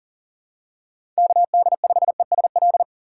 Morse code, heard as a keyed tone, about 40 wpm, 700 Hz. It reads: KD5ESL
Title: USA Call Signs - 3x Character Spacing - 40wpm